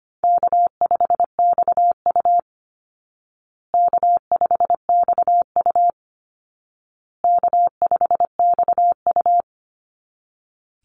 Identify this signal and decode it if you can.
Morse code, heard as a keyed tone, about 25 wpm, 700 Hz. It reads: K5XU K5XU K5XU
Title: Easy Call Signs Repeated 3x - 25wpm